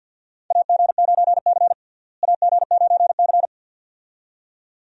Morse code, heard as a keyed tone, about 49 wpm, 700 Hz. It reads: AG9C AG9C
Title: USA Call Signs - 4 Characters - Repeated 2x - 50wpm